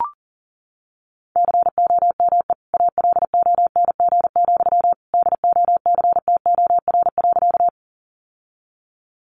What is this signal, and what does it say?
Morse code, heard as a keyed tone, about 40 wpm, 700 Hz. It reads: COME ALONG, DOCTOR.
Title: Sentences from Top 1500 Words 40wpm